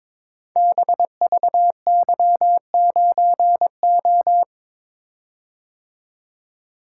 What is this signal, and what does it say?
Morse code, heard as a keyed tone, about 23 wpm, 700 Hz. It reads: BVY9O
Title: Code Groups - 5 Letter:Numbers - 22wpm